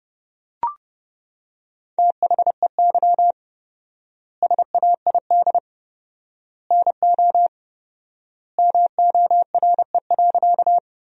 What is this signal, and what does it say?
Morse code, heard as a keyed tone, about 30 wpm, 700 Hz. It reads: THEY SAID NO MORE.
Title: 4x Word-Spacing - Sentences from Top 100 Words - 30wpm